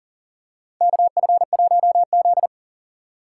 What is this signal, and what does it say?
Morse code, heard as a keyed tone, about 40 wpm, 700 Hz. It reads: KF1Z
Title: USA Call Signs - 4x Character Spacing - 40wpm